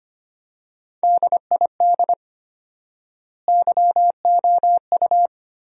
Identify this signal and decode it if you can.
Morse code, heard as a keyed tone, about 25 wpm, 700 Hz. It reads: DID YOU
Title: 4x Word-Spacing - Sentences from Top 100 Words - 25wpm